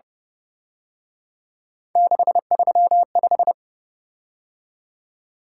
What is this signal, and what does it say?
Morse code, heard as a keyed tone, about 30 wpm, 700 Hz. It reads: 635